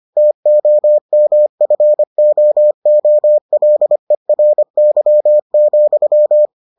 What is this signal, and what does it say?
Morse code, heard as a keyed tone, about 25 wpm, 600 Hz. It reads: TOMFOOLERY,